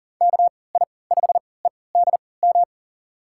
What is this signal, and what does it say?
Morse code, heard as a keyed tone, about 40 wpm, spaced as for 23 wpm, 700 Hz. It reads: KI5EDM
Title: USA Call Signs - 6 Characters - 3x Character Spacing - 40wpm